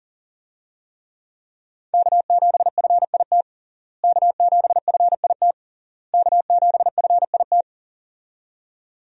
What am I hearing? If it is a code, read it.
Morse code, heard as a keyed tone, about 40 wpm, 700 Hz. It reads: K7FIT K7FIT K7FIT
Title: USA Call Signs - 5 Characters - Repeated 3x - 40wpm